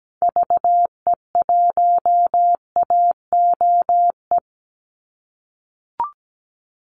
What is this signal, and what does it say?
Morse code, heard as a keyed tone, about 17 wpm, 700 Hz. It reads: VE1AOE
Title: Call Signs - 17wpm